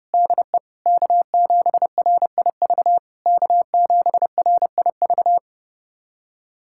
Morse code, heard as a keyed tone, about 32 wpm, 700 Hz. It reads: DE K7RIV K7RIV
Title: SOTA QSO Practice 30wpm